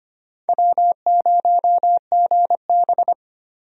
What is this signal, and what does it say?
Morse code, heard as a keyed tone, about 25 wpm, 700 Hz. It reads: W0GB
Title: USA Call Signs - 4 Characters - 25wpm